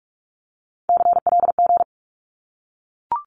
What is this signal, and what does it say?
Morse code, heard as a keyed tone, about 44 wpm, 700 Hz. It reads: CLG
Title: Abbreviations Repeated 3x - 45wpm